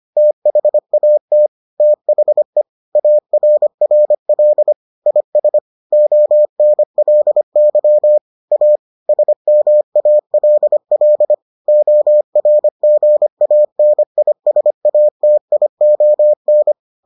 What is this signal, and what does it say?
Morse code, heard as a keyed tone, about 25 wpm, 600 Hz. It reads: THAT THE ARRL IS ONLY A SMALL ORGANISATION